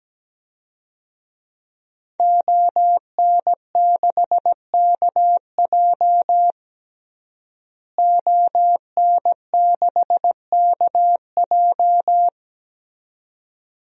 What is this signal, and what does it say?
Morse code, heard as a keyed tone, about 17 wpm, 700 Hz. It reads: ON6KJ ON6KJ
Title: DXCC-100 Call Signs - 5 Characters - Repeated 2x - 17wpm